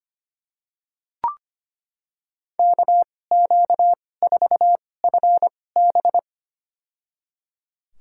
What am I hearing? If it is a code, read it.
Morse code, heard as a keyed tone, about 25 wpm, spaced as for 18 wpm, 700 Hz. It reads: KQ4FB
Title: USA Call Signs - 5 Characters - 2x Character Spacing - 25wpm